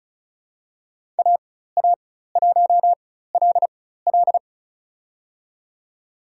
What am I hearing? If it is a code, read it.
Morse code, heard as a keyed tone, about 34 wpm, spaced as for 16 wpm, 700 Hz. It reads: AA1LL
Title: USA Call Signs - 5 Characters - 4x Character Spacing - 35wpm